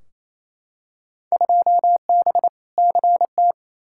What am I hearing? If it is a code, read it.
Morse code, heard as a keyed tone, about 28 wpm, 700 Hz. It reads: 2B CT
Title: Field Day Exchanges Encore - 28wpm